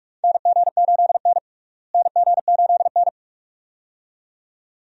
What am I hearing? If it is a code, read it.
Morse code, heard as a keyed tone, about 45 wpm, 700 Hz. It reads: NG8N NG8N
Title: USA Call Signs - 4 Characters - Repeated 2x - 45wpm